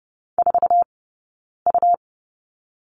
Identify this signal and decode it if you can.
Morse code, heard as a keyed tone, about 30 wpm, 700 Hz. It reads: 4 U
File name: course / 30:10wpm / 099 Lesson 17d - Letter u - Two Characters 30wpm